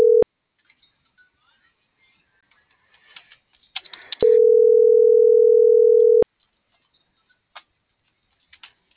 Ringback tone (North America), 440 + 480 Hz.